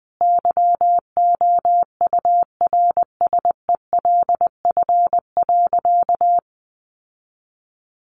Morse code, heard as a keyed tone, about 20 wpm, 700 Hz. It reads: YOURSELF.
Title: Sentences from Top 600 Words 20wpm